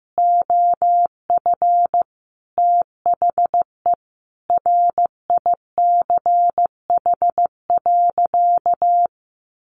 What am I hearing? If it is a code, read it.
Morse code, heard as a keyed tone, about 15 wpm, 700 Hz. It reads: OF THE RICH.
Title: Sentences from Top 1100 Words 15wpm